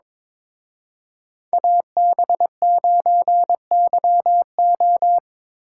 Morse code, heard as a keyed tone, about 26 wpm, 700 Hz. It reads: AB9YO
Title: Speed Racing - USA Call Signs 22wpm